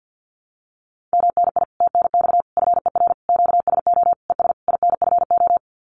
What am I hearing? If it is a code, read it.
Morse code, heard as a keyed tone, about 48 wpm, 700 Hz. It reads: MNI TNX FER QSO ES INFO
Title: Common QSO Phrases Encore 50wpm